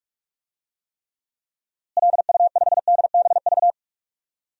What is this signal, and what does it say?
Morse code, heard as a keyed tone, about 46 wpm, 700 Hz. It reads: LU5DBV
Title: DXCC-100 Call Signs - 6 Characters - Repeated 3x - 45wpm